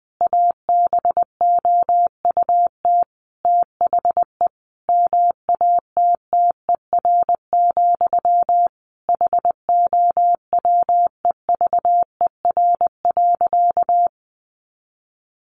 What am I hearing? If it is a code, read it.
Morse code, heard as a keyed tone, about 20 wpm, 700 Hz. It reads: ABOUT THE MATTER, HOWEVER.
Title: Sentences from Top 700 Words 20wpm